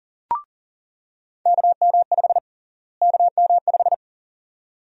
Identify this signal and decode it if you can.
Morse code, heard as a keyed tone, about 40 wpm, 700 Hz. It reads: KM5 KM5
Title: USA Call Sign Prefixes - Repeated 2x - 40wpm